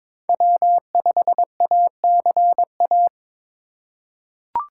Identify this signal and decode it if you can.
Morse code, heard as a keyed tone, about 22 wpm, 700 Hz. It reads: W5ACA
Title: USA Call Signs - 5 Characters - 2x Character Spacing - 22wpm